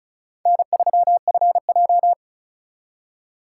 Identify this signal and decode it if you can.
Morse code, heard as a keyed tone, about 44 wpm, 700 Hz. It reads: N3FJ